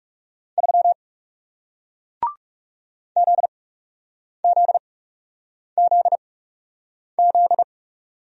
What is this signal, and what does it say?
Morse code, heard as a keyed tone, about 40 wpm, spaced as for 8 wpm, 700 Hz. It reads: 3 ZZZZ